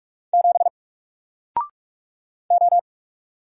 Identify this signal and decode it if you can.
Morse code, heard as a keyed tone, about 46 wpm, 700 Hz. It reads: 7 O
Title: Single Letter-Number Alt 45wpm